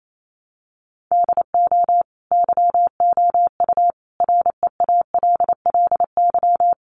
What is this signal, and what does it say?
Morse code, heard as a keyed tone, about 28 wpm, 700 Hz. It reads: DO YOU REALLY
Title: Sentences from Top 600 Words 28wpm